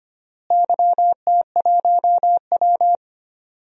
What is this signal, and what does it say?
Morse code, heard as a keyed tone, about 25 wpm, 700 Hz. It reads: YT1W
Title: DXCC-100 Call Signs - 4 Characters - Repeated 2x - 25wpm